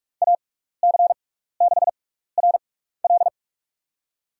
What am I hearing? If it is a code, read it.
Morse code, heard as a keyed tone, about 44 wpm, spaced as for 15 wpm, 700 Hz. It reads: AC6RL
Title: USA Call Signs - 6x Character Spacing - 45wpm